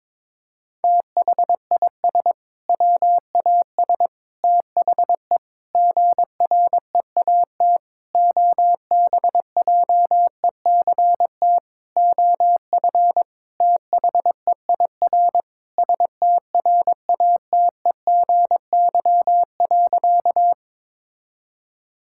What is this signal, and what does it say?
Morse code, heard as a keyed tone, about 22 wpm, 700 Hz. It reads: THIS WAS THE GREAT OBJECT OF THEIR STRATEGY.
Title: Sentences from Top 1000 Words - Review Part 5 - 22wpm